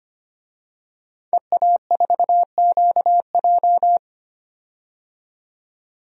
Morse code, heard as a keyed tone, about 25 wpm, 700 Hz. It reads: EA4QJ
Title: DXCC-100 Call Signs - 5 Characters - 25wpm